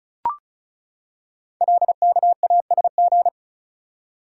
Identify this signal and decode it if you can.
Morse code, heard as a keyed tone, about 36 wpm, 700 Hz. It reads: LKASG
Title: Code Groups - 5 Letters - 35wpm